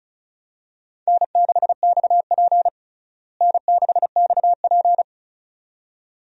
Morse code, heard as a keyed tone, about 36 wpm, 700 Hz. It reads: N6XP N6XP
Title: USA Call Signs - 4 Characters - Repeated 2x - 35wpm